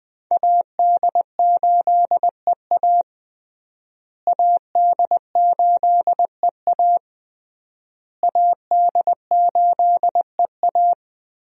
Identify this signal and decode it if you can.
Morse code, heard as a keyed tone, about 20 wpm, 700 Hz. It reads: AD8EA AD8EA AD8EA
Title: USA Call Signs - 5 Characters - Repeated 3x - 20wpm